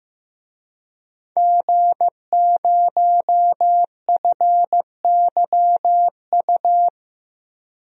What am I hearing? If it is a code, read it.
Morse code, heard as a keyed tone, about 15 wpm, 700 Hz. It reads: G0FYU
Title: DXCC-100 Call Signs - 5 Characters - Repeated 2x - 15wpm